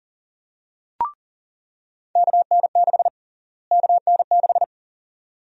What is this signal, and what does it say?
Morse code, heard as a keyed tone, about 40 wpm, 700 Hz. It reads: KN6 KN6
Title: USA Call Sign Prefixes - Repeated 2x - 40wpm